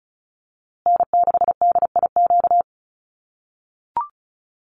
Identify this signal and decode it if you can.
Morse code, heard as a keyed tone, about 34 wpm, 700 Hz. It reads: N6DIQ
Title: USA Call Signs Repeated 3x 35wpm